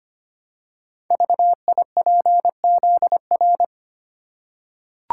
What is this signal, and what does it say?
Morse code, heard as a keyed tone, about 25 wpm, 700 Hz. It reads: VIPZR
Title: Code Groups - 5 Letters - 25wpm